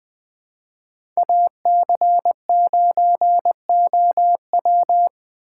Morse code, heard as a keyed tone, about 21 wpm, 700 Hz. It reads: AC9OW
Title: USA Call Signs - 6x Character Spacing - 20wpm